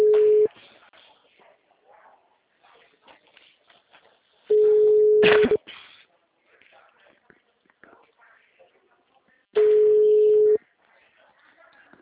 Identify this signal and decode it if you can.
Ringback tone (Europe, 425 Hz), with speech.